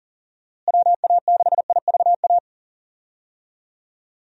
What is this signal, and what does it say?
Morse code, heard as a keyed tone, about 40 wpm, 700 Hz. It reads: WA6IVA
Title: USA Call Signs - 6 Characters - Repeated 3x - 40wpm